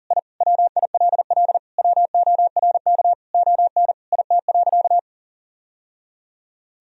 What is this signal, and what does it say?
Morse code, heard as a keyed tone, about 40 wpm, 700 Hz. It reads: I WILL WORK ON IT.
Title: Sentences from Top 1000 Words - Review Part 3 - 40wpm